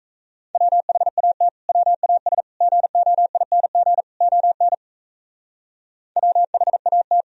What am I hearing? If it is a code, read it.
Morse code, heard as a keyed tone, about 34 wpm, 700 Hz. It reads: WHAT WAS GOING ON WHAT
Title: Speed-Racing - Sets of 4 Words 28wpm